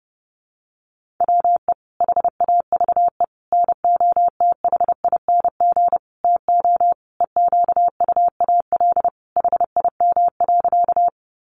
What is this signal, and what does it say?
Morse code, heard as a keyed tone, about 30 wpm, 700 Hz. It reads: WE HAVE NOTHING TO EQUAL HIM.
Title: Sentences from Top 1400 Words 30wpm